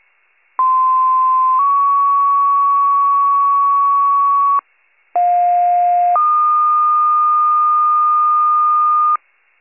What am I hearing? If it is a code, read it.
Two-tone page, heard as a one-second tone, then a three-second tone, about 1040 then 1120 Hz, about 710 then 1220 Hz. It is heard from 0.6 to 9.2 s.
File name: pager / 46 RESCUE / wcalert